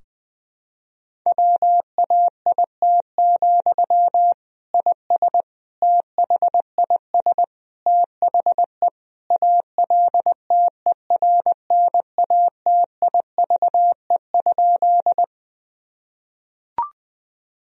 Morse code, heard as a keyed tone, about 21 wpm, 700 Hz. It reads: WAIT, IS THIS THE ALTERNATIVE?
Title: Sentences from Top 1000 Words - Review Part 4 - 20wpm